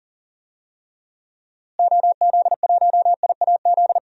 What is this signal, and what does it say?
Morse code, heard as a keyed tone, about 40 wpm, 700 Hz. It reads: OZ1IAZ
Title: DXCC-100 Call Signs - 6 Characters - Repeated 2x - 40wpm